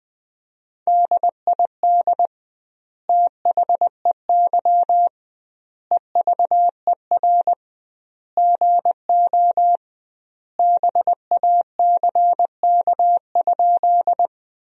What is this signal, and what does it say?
Morse code, heard as a keyed tone, about 20 wpm, 700 Hz. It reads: DID THEY EVER GO BACK?
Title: 2x Word-Spacing - Sentences from Top 400 Words - 20wpm